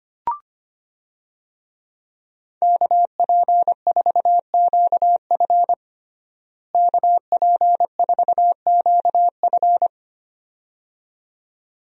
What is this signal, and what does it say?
Morse code, heard as a keyed tone, about 25 wpm, 700 Hz. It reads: KP4QF KP4QF
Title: DXCC-100 Call Signs - 5 Characters - Repeated 2x - 25wpm